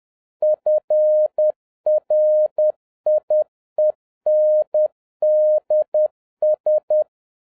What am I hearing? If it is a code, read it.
Morse code, heard as a keyed tone, about 10 wpm, 610 Hz. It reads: FRIENDS
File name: 2011-07-mp3 / n46 JOHN K3TN 10wpm